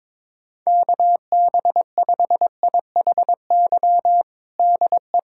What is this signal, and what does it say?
Morse code, heard as a keyed tone, about 22 wpm, 700 Hz. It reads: KB5IHY DE
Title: USA Callsign DE Callsign - 6 Characters - 22wpm